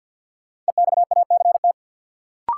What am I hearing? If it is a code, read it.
Morse code, heard as a keyed tone, about 53 wpm, 700 Hz. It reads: EXACT
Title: Sentences from Top 1300 Words - New Word Drill Alt - 50wpm